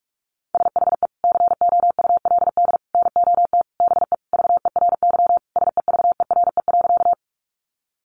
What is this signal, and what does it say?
Morse code, heard as a keyed tone, about 44 wpm, 700 Hz. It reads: SHE COULD NOT BE VERY SEVERE.
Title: Sentences from Top 1400 Words 45wpm